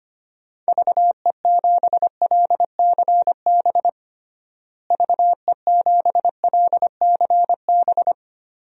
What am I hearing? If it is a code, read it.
Morse code, heard as a keyed tone, about 25 wpm, 700 Hz. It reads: VE7LCB VE7LCB
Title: DXCC-100 Call Signs - 6 Characters - Repeated 3x - 25wpm